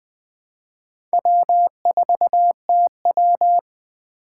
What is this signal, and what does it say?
Morse code, heard as a keyed tone, about 20 wpm, 700 Hz. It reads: W4TW